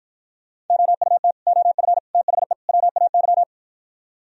Morse code, heard as a keyed tone, about 56 wpm, 700 Hz. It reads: OUT OF THE WAY